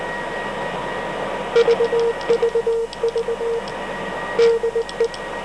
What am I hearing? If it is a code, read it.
Morse code, heard as a keyed tone, about 18 wpm, 460 Hz, with noise. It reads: VVV DE
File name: FUF 16961 1936